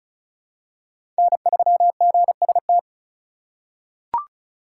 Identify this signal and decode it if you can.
Morse code, heard as a keyed tone, about 36 wpm, 700 Hz. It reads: N3GST
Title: USA Call Signs - 5 Characters - 4x Character Spacing - 35wpm